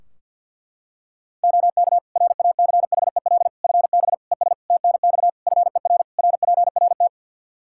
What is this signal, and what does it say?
Morse code, heard as a keyed tone, about 46 wpm, 700 Hz. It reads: OK RACHEL FB ES TNX FER RPRT
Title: Practice QSO - Exchange 2 - Info - 50wpm